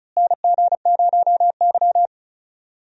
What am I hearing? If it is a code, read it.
Morse code, heard as a keyed tone, about 36 wpm, 700 Hz. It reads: NG0Y
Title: USA Call Signs - 4 Characters - Repeated 2x - 35wpm